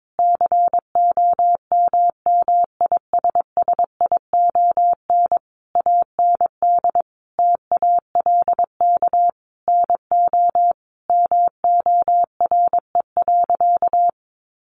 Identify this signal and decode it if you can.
Morse code, heard as a keyed tone, about 22 wpm, 700 Hz. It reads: COMMISSION AND TALK NO MORE.
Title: Sentences from Top 1400 Words 22wpm